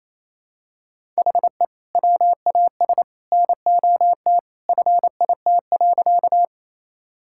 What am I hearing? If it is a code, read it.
Morse code, heard as a keyed tone, about 28 wpm, 700 Hz. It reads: HE WAS NOT FIT.
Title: Sentences from Top 1000 Words - Review Part 5 - 28wpm